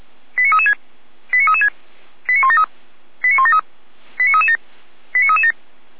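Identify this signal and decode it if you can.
Five-tone selective call (ZVEI), heard as quick runs of five tones, tones of about 70 ms. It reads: 89336 89336 89173 89173 89337 89337
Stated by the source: ZVEI five-tone selective call received with an SDR, from Feuerwehr Orlamünde.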